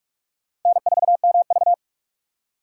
Morse code, heard as a keyed tone, about 44 wpm, 700 Hz. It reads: N4MV